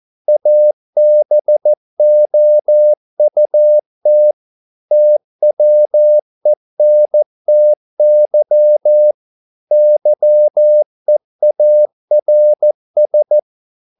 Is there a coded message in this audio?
Yes: Morse code from 0.3 to 13.4 s, heard as a keyed tone, about 14 wpm, 600 Hz. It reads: ABOUT TWENTY YEARS